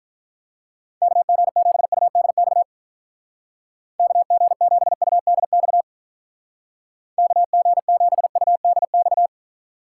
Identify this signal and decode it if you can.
Morse code, heard as a keyed tone, about 43 wpm, 700 Hz. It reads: KG7UDX KG7UDX KG7UDX
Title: Speed Racing - USA 2x3 Call Signs 35wpm